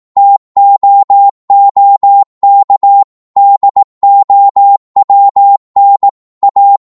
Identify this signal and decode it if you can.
Morse code, heard as a keyed tone, about 18 wpm, 800 Hz. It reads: TOOK DOWN A
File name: Morse / psg-18wpm